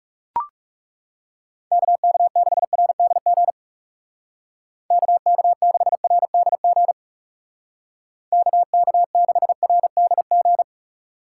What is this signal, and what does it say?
Morse code, heard as a keyed tone, about 35 wpm, 700 Hz. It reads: KK6RDG KK6RDG KK6RDG
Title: Speed Racing - USA 2x3 Call Signs 30wpm